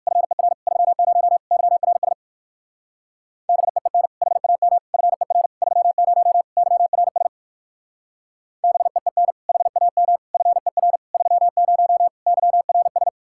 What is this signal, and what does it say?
Morse code, heard as a keyed tone, about 54 wpm, 700 Hz. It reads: FER 30 YRS BEEN HAM FER 30 YRS BEEN HAM FER 30 YRS
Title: Speed-Racing Common QSO Phrases - 45wpm